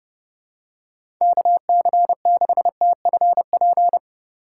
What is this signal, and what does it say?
Morse code, heard as a keyed tone, about 29 wpm, 690 Hz. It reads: KC6TFP
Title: USA Call Signs - 3x Character Spacing - 30wpm